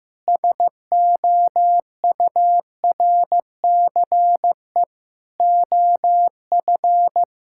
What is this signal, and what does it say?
Morse code, heard as a keyed tone, about 15 wpm, 700 Hz. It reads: SOURCE OF
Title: Sentences from Top 1000 Words - Review Part 5 - 15wpm